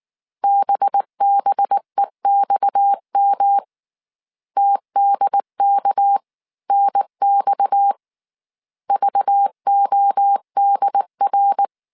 Morse code, heard as a keyed tone, about 19 wpm, 800 Hz. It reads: BBEXM TDK NX VODR
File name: CW / DET / 1 6 1500C S C 18wpm 2356